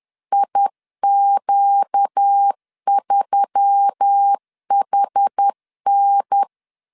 Morse code, heard as a keyed tone, about 10 wpm, 800 Hz. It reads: IQ3HN
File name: CW / DET / 1 6 1500C S C 10wpm 4252